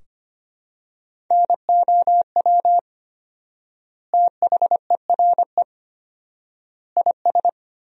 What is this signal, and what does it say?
Morse code, heard as a keyed tone, about 25 wpm, 700 Hz. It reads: NOW THERE IS
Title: 4x Word-Spacing - Sentences from Top 100 Words - 25wpm